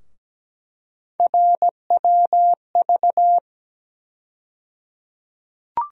Morse code, heard as a keyed tone, about 17 wpm, 700 Hz. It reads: RWV
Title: USA Call Sign Postfixes - 3 Letters 17wpm